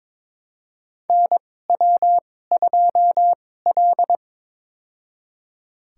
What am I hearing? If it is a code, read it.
Morse code, heard as a keyed tone, about 22 wpm, spaced as for 16 wpm, 700 Hz. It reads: NW2L
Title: USA Call Signs - 4 Characters - 2x Character Spacing - 22wpm